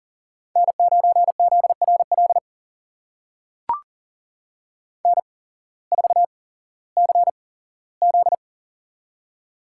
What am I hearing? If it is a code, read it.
Morse code, heard as a keyed tone, about 40 wpm, spaced as for 19 wpm, 700 Hz. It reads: N9ZRL N4CZ
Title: USA Call Signs - 8x Character Spacing - 40wpm